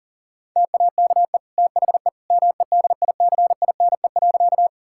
Morse code, heard as a keyed tone, about 40 wpm, 700 Hz. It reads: TAKE THE MEDICINE.
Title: Sentences from Top 1000 Words - Review Part 6 - 40wpm